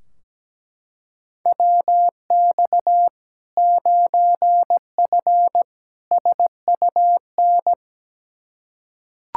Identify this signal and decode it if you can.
Morse code, heard as a keyed tone, about 17 wpm, 700 Hz. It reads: WX 9F SUN
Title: Practice QSO - Exchange 2 - Info - 17wpm